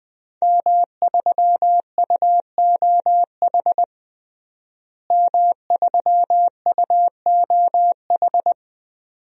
Morse code, heard as a keyed tone, about 20 wpm, 700 Hz. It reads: M3UOH M3UOH
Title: DXCC-100 Call Signs - 5 Characters - Repeated 2x - 20wpm